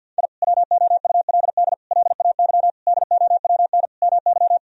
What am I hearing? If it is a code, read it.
Morse code, heard as a keyed tone, about 50 wpm, 700 Hz. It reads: I WOULD LAY DOWN MY
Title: Sentences from Top 1000 Words - Review Part 2 - 50wpm